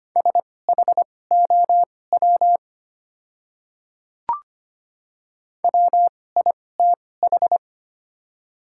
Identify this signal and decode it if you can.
Morse code, heard as a keyed tone, about 25 wpm, spaced as for 18 wpm, 700 Hz. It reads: SHOW WITH
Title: 2x Character Spacing - Top 200 Words - 25wpm